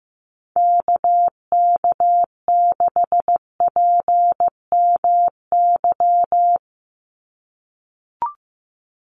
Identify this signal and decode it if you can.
Morse code, heard as a keyed tone, about 15 wpm, 700 Hz. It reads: KK6PMY
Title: USA Call Signs 15wpm